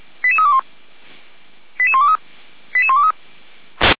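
Five-tone selective call (ZVEI), heard as quick runs of five tones, tones of about 70 ms. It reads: 88321 88123 88123
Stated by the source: ZVEI five-tone call received by SDR, from audio.f85360.feuerwehr-orlamuende.org (Feuerwehr Orlamünde fire brigade).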